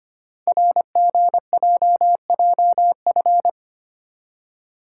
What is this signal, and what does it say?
Morse code, heard as a keyed tone, about 25 wpm, 700 Hz. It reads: RGJJF
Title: Code Groups - 5 Letters - 25wpm